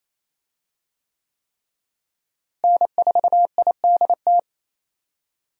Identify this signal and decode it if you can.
Morse code, heard as a keyed tone, about 28 wpm, 700 Hz. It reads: N4IDT
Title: USA Call Signs - 5 Characters - Repeated 3x - 28wpm